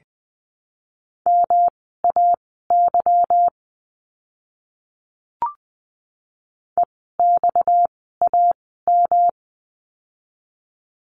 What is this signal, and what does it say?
Morse code, heard as a keyed tone, about 20 wpm, spaced as for 14 wpm, 700 Hz. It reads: MAY EXAM
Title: 230 Lesson 37f - Letter x - Words 20wpm